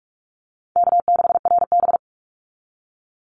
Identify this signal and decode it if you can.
Morse code, heard as a keyed tone, about 44 wpm, 700 Hz. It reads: K6RB
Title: Easy Call Signs Repeated 3x - 45wpm